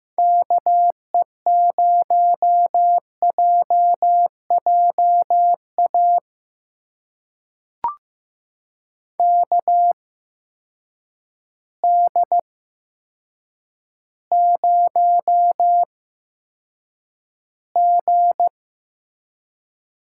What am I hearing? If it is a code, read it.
Morse code, heard as a keyed tone, about 15 wpm, spaced as for 8 wpm, 700 Hz. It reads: KE0JJA KD0G
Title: USA Call Signs - 6 Characters - 8x Character Spacing - 15wpm